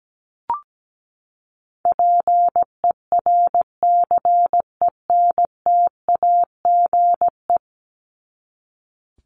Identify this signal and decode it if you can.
Morse code, heard as a keyed tone, about 17 wpm, 700 Hz. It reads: PERCENTAGE
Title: Top 1001-1250 Words 17wpm